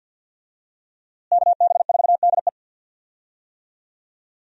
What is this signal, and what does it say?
Morse code, heard as a keyed tone, about 47 wpm, 700 Hz. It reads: KB4DE
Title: USA Call Signs - 5 Characters - Repeated 3x - 50wpm